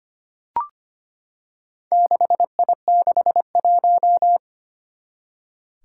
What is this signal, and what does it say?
Morse code, heard as a keyed tone, about 25 wpm, 700 Hz. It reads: 6I61
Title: Four Challenging Letter-Number Combinations 25wpm